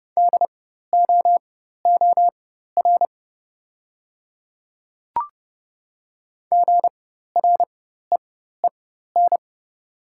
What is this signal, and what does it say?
Morse code, heard as a keyed tone, about 31 wpm, spaced as for 14 wpm, 700 Hz. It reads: DOOR GREEN
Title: 4x Character Spacing - Top 500 Words - 30wpm